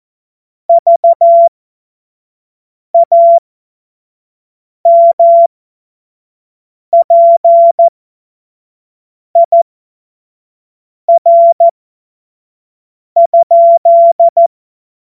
Morse code, heard as a keyed tone, about 14 wpm, spaced as for 5 wpm, 680 Hz. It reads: VAMPIR?